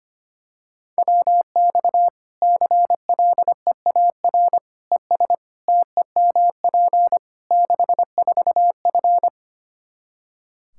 Morse code, heard as a keyed tone, about 25 wpm, 700 Hz. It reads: WX CLEAR ES TEMP 64F
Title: K1USN SST QSO Practice 25wpm